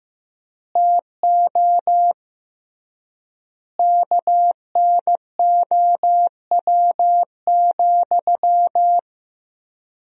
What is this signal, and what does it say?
Morse code, heard as a keyed tone, about 15 wpm, 700 Hz. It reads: TO KNOW,
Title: 3x Word-Spacing - Sentences from Top 200 Words - 15wpm